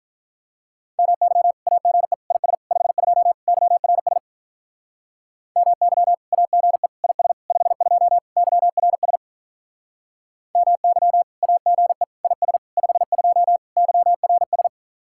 Morse code, heard as a keyed tone, about 43 wpm, 700 Hz. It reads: MY AGE IS 52 YRS MY AGE IS 52 YRS MY AGE IS 52 YRS
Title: Speed-Racing Common QSO Phrases Encore - 35wpm